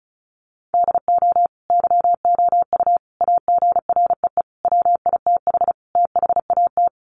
Morse code, heard as a keyed tone, about 34 wpm, 700 Hz. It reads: DO YOU AGREE WITH THAT
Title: Sets of 5 Words Encore 35wpm